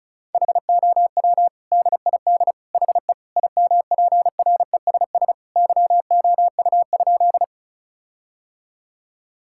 Morse code, heard as a keyed tone, about 36 wpm, 700 Hz. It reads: HOW DID HE IMPRESS YOU?
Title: Sentences from Top 1600 Words 35wpm